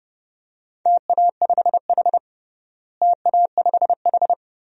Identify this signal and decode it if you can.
Morse code, heard as a keyed tone, about 30 wpm, 700 Hz. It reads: TA5H TA5H
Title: DXCC-100 Call Signs - 4 Characters - Repeated 2x - 30wpm